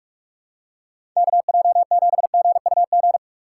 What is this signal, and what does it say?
Morse code, heard as a keyed tone, about 41 wpm, 700 Hz. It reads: KJ7GUG
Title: USA Call Signs - 6 Characters - 6x Character Spacing - 45wpm